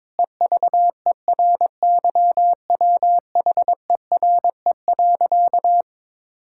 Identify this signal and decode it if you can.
Morse code, heard as a keyed tone, about 22 wpm, 700 Hz. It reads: EVERYWHERE.
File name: practice / 22wpm / Sentences with 8-Word Phrases - 22wpm